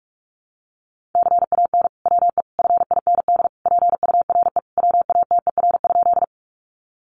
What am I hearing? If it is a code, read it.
Morse code, heard as a keyed tone, about 44 wpm, 700 Hz. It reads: CAN WE FIND PURE WATER?
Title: Sentences from Top 1500 Words 45wpm